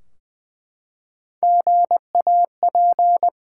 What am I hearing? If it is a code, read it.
Morse code, heard as a keyed tone, about 20 wpm, 700 Hz. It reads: GAP
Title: Three Letter Words - 20wpm